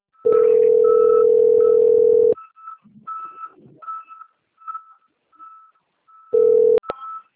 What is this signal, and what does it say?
Ringback tone (North America), 440 + 480 Hz.
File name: out- - - - - .117